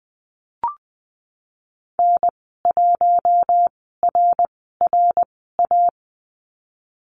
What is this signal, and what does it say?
Morse code, heard as a keyed tone, about 20 wpm, spaced as for 15 wpm, 700 Hz. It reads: N1RRA
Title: 055 Lesson 10g - Letter r - Call Signs 20wpm